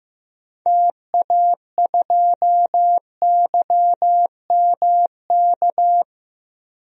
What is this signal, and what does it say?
Morse code, heard as a keyed tone, about 16 wpm, 700 Hz. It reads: TA2YMK
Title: DXCC-100 Call Signs - 6 Characters - 15wpm